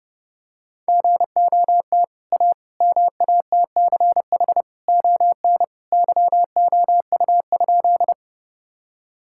Morse code, heard as a keyed tone, about 28 wpm, 700 Hz. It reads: GOT A MATCH ON YOU?
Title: Sentences from Top 1600 Words 30wpm